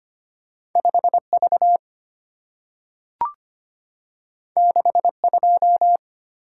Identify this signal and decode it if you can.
Morse code, heard as a keyed tone, about 25 wpm, 700 Hz. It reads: 5V 62
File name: Two Letter-Number Combinations - 25wpm